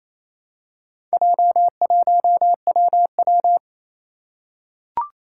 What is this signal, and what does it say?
Morse code, heard as a keyed tone, about 28 wpm, 700 Hz. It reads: J1WW
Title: Four Challenging Letter-Number Combinations 28wpm